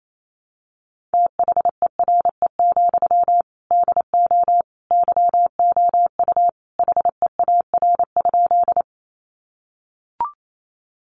Morse code, heard as a keyed tone, about 28 wpm, 700 Hz. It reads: THERE, DO YOU HEAR?
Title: Sentences from Top 300 Words 28wpm